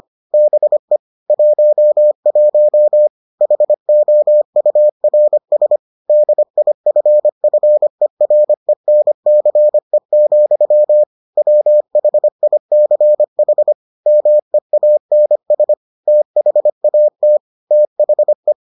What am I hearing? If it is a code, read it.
Morse code, heard as a keyed tone, about 25 wpm, 600 Hz. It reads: BE 11 HOURS DIFFERENCE, WHICH MEANS THAT THE